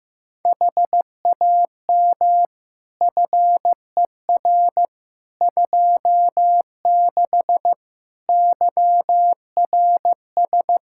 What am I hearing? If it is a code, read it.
Morse code, heard as a keyed tone, about 15 wpm, 700 Hz. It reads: HAM FER 26 YRS
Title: Practice QSO - Exchange 3 - More Info - 15wpm